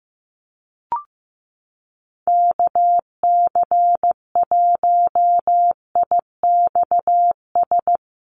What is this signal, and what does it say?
Morse code, heard as a keyed tone, about 15 wpm, 700 Hz. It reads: KC1IXS
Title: USA Call Signs Repeated 3x 15wpm